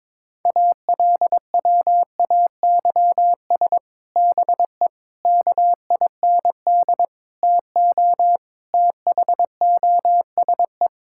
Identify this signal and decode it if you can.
Morse code, heard as a keyed tone, about 22 wpm, 700 Hz. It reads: ALWAYS BE KIND TO THOSE
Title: Sentences from Top 1600 Words 22wpm